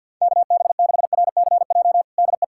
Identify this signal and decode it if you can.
Morse code, heard as a keyed tone, about 52 wpm, 700 Hz. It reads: KB6RCJ DE
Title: USA Callsign DE Callsign - 6 Characters - 50wpm